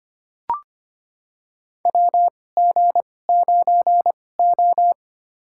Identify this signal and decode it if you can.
Morse code, heard as a keyed tone, about 25 wpm, spaced as for 18 wpm, 700 Hz. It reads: WG9O
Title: USA Call Signs - 4 Characters - 2x Character Spacing - 25wpm